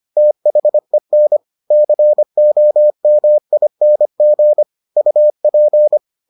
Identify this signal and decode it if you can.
Morse code, heard as a keyed tone, about 25 wpm, 600 Hz. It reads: THEN COMING UP